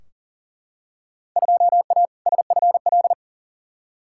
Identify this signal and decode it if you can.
Morse code, heard as a keyed tone, about 40 wpm, 700 Hz. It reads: 2A SFL
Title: Field Day Exchanges Encore - 40wpm